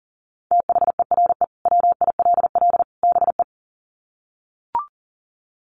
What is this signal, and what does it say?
Morse code, heard as a keyed tone, about 40 wpm, 700 Hz. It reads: THERE WILL BE
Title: Sets of 3 Words 40wpm